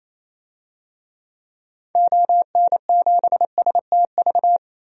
Morse code, heard as a keyed tone, about 27 wpm, 700 Hz. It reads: ON7STV
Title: DXCC-100 Call Signs - 6 Characters - Repeated 2x - 28wpm